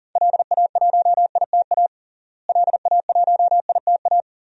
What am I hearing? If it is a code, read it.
Morse code, heard as a keyed tone, about 40 wpm, 700 Hz. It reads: LA1ITA LA1ITA
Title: DXCC-100 Call Signs - 6 Characters - Repeated 3x - 40wpm